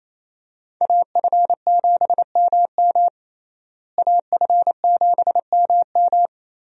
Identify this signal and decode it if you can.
Morse code, heard as a keyed tone, about 28 wpm, 700 Hz. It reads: AF7MM AF7MM
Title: USA Call Signs - 5 Characters - Repeated 2x - 28wpm